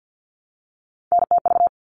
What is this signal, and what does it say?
Morse code, heard as a keyed tone, about 48 wpm, 700 Hz. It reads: NTV
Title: All Cut Numbers - 3-Digit 50wpm